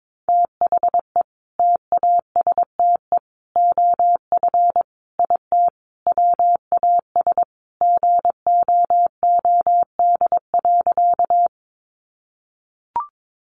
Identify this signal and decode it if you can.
Morse code, heard as a keyed tone, about 22 wpm, 700 Hz. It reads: THE TASTE OF IT WAS GOOD.
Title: Sentences from Top 1100 Words 22wpm